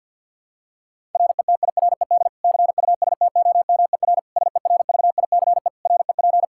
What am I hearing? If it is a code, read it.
Morse code, heard as a keyed tone, about 53 wpm, 700 Hz. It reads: RETIRED CUSTOMER SERVICE REP
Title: Practice QSO - Exchange 3 - More Info - 50wpm